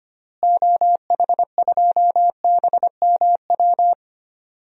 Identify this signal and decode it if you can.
Morse code, heard as a keyed tone, about 25 wpm, 700 Hz. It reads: OH2BMW
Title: DXCC-100 Call Signs - 6 Characters - Repeated 2x - 25wpm